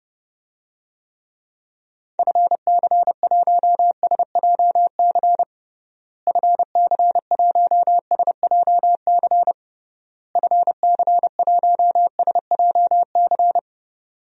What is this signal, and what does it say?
Morse code, heard as a keyed tone, about 32 wpm, 700 Hz. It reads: FC1SJC FC1SJC FC1SJC
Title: DXCC-100 Call Signs - 6 Characters - Repeated 3x - 30wpm